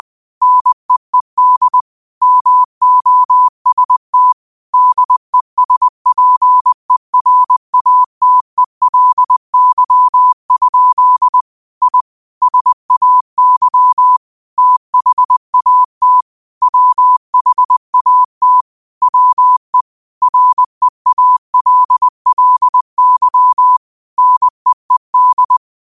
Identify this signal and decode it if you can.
Morse code, heard as a keyed tone, about 20 wpm, 1000 Hz. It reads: NEED MOST DESPERATELY? I SAY THAT WHAT WE REALLY NEED
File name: qotdfiles / 20wpm